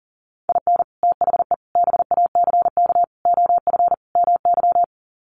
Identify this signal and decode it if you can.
Morse code, heard as a keyed tone, about 40 wpm, 700 Hz. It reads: IN THE BACK OF MY
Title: Sets of 5 Words Encore 40wpm